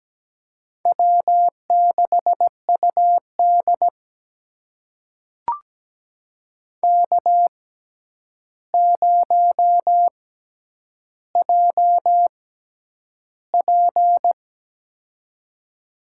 Morse code, heard as a keyed tone, about 17 wpm, spaced as for 9 wpm, 700 Hz. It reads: W6UD K0JP
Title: USA Call Signs - 4 Characters - 6x Character Spacing - 17wpm